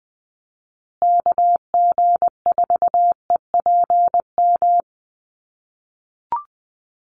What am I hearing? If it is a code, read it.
Morse code, heard as a keyed tone, about 20 wpm, 700 Hz. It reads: KG4EPM
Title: USA Call Signs Repeated 2x 20wpm